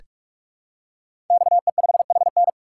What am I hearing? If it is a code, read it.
Morse code, heard as a keyed tone, about 44 wpm, 700 Hz. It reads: XE5HN
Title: Code Groups - 5 Letter:Numbers - 45wpm